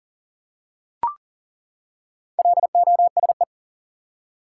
Morse code, heard as a keyed tone, about 40 wpm, 700 Hz. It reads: LOSE
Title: Four Letter Words - 40wpm